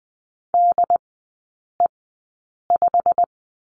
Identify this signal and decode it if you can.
Morse code, heard as a keyed tone, about 20 wpm, spaced as for 8 wpm, 700 Hz. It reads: DE5
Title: 087 Lesson 15e - Number 5 - Three Characters 20wpm